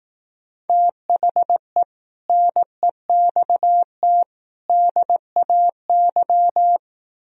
Morse code, heard as a keyed tone, about 18 wpm, 700 Hz. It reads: THE NEXT DAY
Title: Speed-Racing - Sets of 3 Words 15wpm